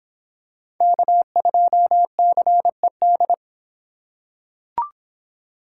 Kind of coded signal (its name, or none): Morse code